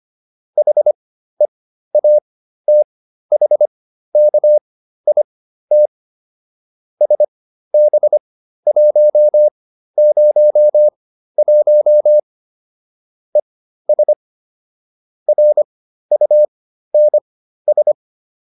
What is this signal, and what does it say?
Morse code, heard as a keyed tone, about 25 wpm, spaced as for 13 wpm, 600 Hz. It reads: HEATHKIT SB101 ES RUNS
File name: QSO310 13